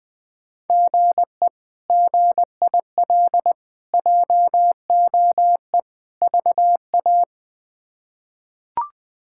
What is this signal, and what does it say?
Morse code, heard as a keyed tone, about 20 wpm, 700 Hz. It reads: GE GIL JOE VA
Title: K1USN SST QSO Practice - 20wpm